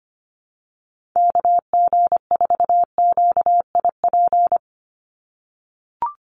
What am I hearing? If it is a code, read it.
Morse code, heard as a keyed tone, about 25 wpm, 700 Hz. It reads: KG4QIP
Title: USA Call Signs 25wpm